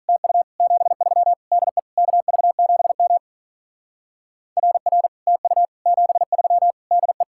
Morse code, heard as a keyed tone, about 43 wpm, 700 Hz. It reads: TU 73 DE KV7M RR TU 73 DE
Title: Speed-Racing Common QSO Phrases - 35wpm